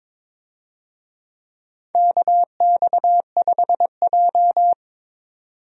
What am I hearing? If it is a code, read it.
Morse code, heard as a keyed tone, about 22 wpm, 700 Hz. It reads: KX5J
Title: USA Call Signs - 4 Characters - 22wpm